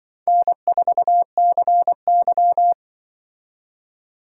Morse code, heard as a keyed tone, about 24 wpm, 700 Hz. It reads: N4CY
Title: Speed Racing - USA 1x2 Call Signs 20wpm